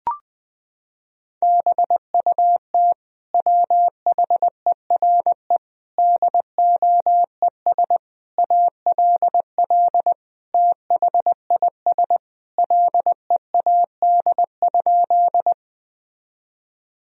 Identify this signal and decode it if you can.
Morse code, heard as a keyed tone, about 20 wpm, 700 Hz. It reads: BUT WHERE DOES ALL THIS LEAD?
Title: Sentences from Top 1000 Words - Review Part 2 - 20wpm